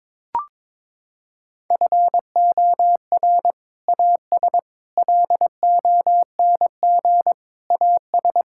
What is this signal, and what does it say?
Morse code, heard as a keyed tone, about 21 wpm, 700 Hz. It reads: FOR AS LONG AS
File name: practice / 22wpm / Sets of 7 Words - 22wpm